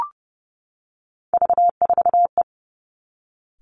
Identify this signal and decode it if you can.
Morse code, heard as a keyed tone, about 30 wpm, 700 Hz. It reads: V4E
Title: Three Challenging Letter-Number Combinations 30wpm